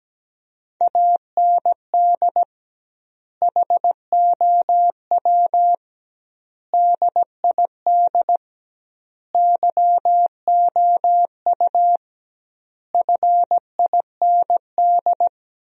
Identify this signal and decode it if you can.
Morse code, heard as a keyed tone, about 17 wpm, 700 Hz. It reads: AND HOW DID YOU FIND
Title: 2x Word-Spacing - Sentences from Top 200 Words - 17wpm